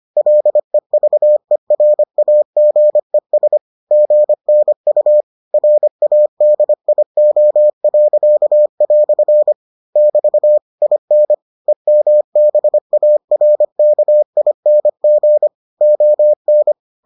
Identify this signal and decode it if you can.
Morse code, heard as a keyed tone, about 25 wpm, 600 Hz. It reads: LEVERAGES GNU RADIO." <BT> IN EMBARKING ON